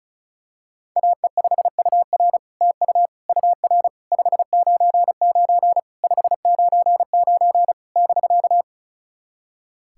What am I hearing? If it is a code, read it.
Morse code, heard as a keyed tone, about 36 wpm, 700 Hz. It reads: AE5UR TU UR 599 599 <BK>
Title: SOTA QSO Practice 35wpm